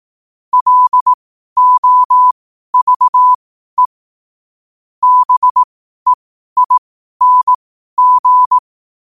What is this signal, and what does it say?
Morse code, heard as a keyed tone, about 18 wpm, spaced as for 12 wpm, 1000 Hz. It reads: LOVE BEING
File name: qotdfiles / 12wpm